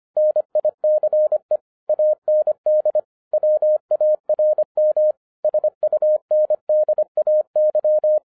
Morse code, heard as a keyed tone, about 25 wpm, 610 Hz. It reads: NICE AND WARM SUNDAY
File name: 2011-07-mp3 / n44 JOHN K3TN 25wpm